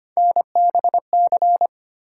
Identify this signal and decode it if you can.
Morse code, heard as a keyed tone, about 25 wpm, 700 Hz. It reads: NBC
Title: Sets of 5 Words Encore - New Word Drill - 25wpm